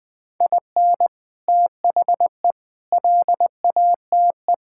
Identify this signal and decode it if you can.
Morse code, heard as a keyed tone, about 20 wpm, 700 Hz. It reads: IN THE LATE
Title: Sentences with 3-Word Phrases - 20wpm